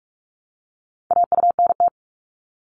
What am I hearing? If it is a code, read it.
Morse code, heard as a keyed tone, about 46 wpm, 700 Hz. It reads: AUNT
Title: Multi-letter Phonemes Repeated 3x 45wpm